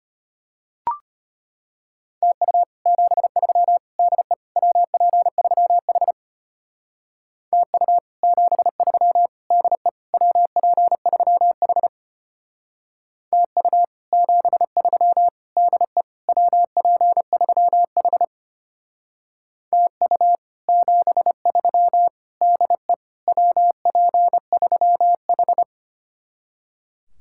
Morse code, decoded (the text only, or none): TU 73 DE WP3H TU 73 DE WP3H TU 73 DE WP3H TU 73 DE WP3H